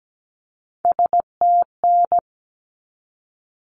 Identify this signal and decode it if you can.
Morse code, heard as a keyed tone, about 17 wpm, 700 Hz. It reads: STN
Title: Abbreviations 17wpm